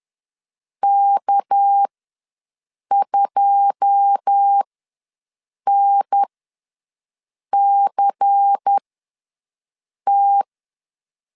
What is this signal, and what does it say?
Morse code, heard as a keyed tone, about 10 wpm, spaced as for 5 wpm, 800 Hz. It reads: K2NCT
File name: LEZIONI / 9 2 ws 10wpm 4130